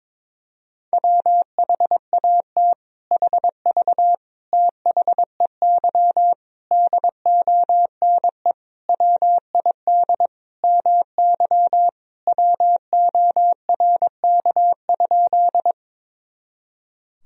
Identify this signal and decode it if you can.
Morse code, heard as a keyed tone, about 22 wpm, 700 Hz. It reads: WHAT HV THEY DONE WID MY WORK?
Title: Sentences from Top 500 Words and Abbreviations 22wpm